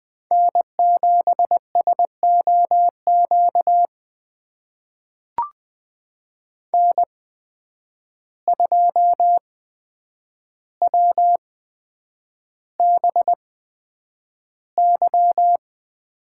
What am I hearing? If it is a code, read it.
Morse code, heard as a keyed tone, about 20 wpm, spaced as for 9 wpm, 700 Hz. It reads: N7SOQ N2WBY
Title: USA Call Signs - 5 Characters - 8x Character Spacing - 20wpm